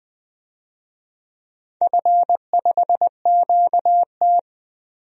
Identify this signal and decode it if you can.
Morse code, heard as a keyed tone, about 20 wpm, 700 Hz. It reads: F5QT
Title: DXCC-100 Call Signs - 4 Characters - Repeated 3x - 20wpm